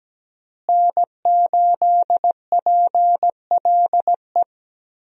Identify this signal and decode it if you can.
Morse code, heard as a keyed tone, about 17 wpm, 700 Hz. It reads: N8PLE
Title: USA Call Signs - 5 Characters - Repeated 3x - 17wpm